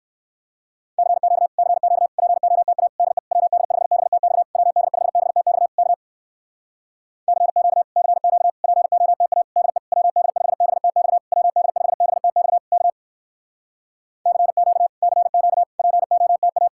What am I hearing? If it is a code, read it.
Morse code, heard as a keyed tone, about 66 wpm, 700 Hz. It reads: CQ CQ POTA DE WD5BTX WD5BTX K CQ CQ POTA DE WD5BTX WD5BTX K CQ CQ POTA